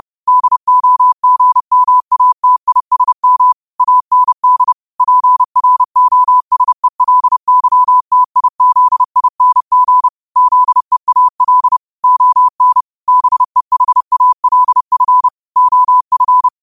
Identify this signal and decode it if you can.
Morse code, heard as a keyed tone, about 30 wpm, 1000 Hz. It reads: DOGMATISM AND PROSELYTIZING ZEAL ON BEHALF OF
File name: qotdfiles / 30wpm